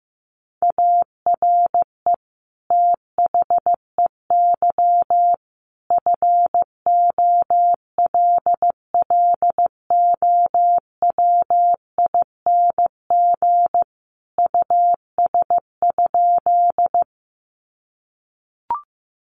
Morse code, heard as a keyed tone, about 15 wpm, 700 Hz. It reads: ARE THEY FOLLOWING US?
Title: Sentences from Top 700 Words 15wpm